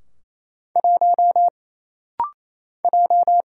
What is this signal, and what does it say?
Morse code, heard as a keyed tone, about 27 wpm, 700 Hz. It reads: 1 J
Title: Single Letter-Number - Rapid-Fire 28wpm